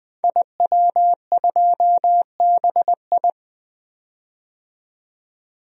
Morse code, heard as a keyed tone, about 20 wpm, 700 Hz. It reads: IW2BI
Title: DXCC-100 Call Signs - 5 Characters - 20wpm